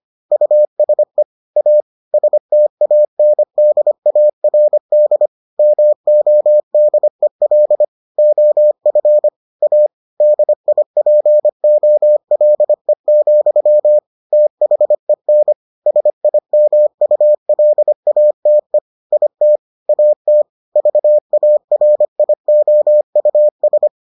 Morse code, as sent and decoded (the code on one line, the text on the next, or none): ..- ... / .- / ... - .- -. -.. .- .-. -.. / -- --- -.. .-.. / --- ..-. / .- / -.. .. .--. --- .-.. --..-- / - .... -. / ... .. -- ..- .-.. .- - . / .. - / .- - / ...- .- .-. .. --- ..- ...
USE A STANDARD MODEL OF A DIPOLE, THEN SIMULATE IT AT VARIOUS